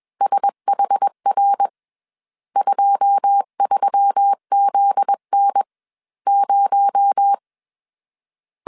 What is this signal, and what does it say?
Morse code, heard as a keyed tone, about 21 wpm, spaced as for 14 wpm, 800 Hz. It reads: SHR23 ZN0